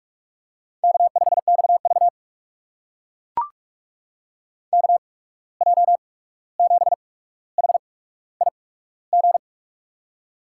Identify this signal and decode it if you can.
Morse code, heard as a keyed tone, about 44 wpm, spaced as for 17 wpm, 700 Hz. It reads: K5XV KJ7HIG
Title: USA Call Signs - 8x Character Spacing - 45wpm